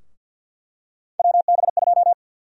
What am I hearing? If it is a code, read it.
Morse code, heard as a keyed tone, about 49 wpm, 700 Hz. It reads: WB2
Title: USA Call Sign Prefixes - 50wpm